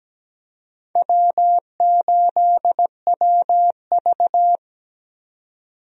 Morse code, heard as a keyed tone, about 17 wpm, 700 Hz. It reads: W8WV
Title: USA Call Signs - 4 Characters - 17wpm